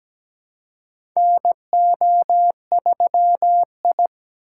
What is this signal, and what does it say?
Morse code, heard as a keyed tone, about 17 wpm, 700 Hz. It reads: NO3I
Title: USA Call Signs - 4 Characters - 2x Character Spacing - 17wpm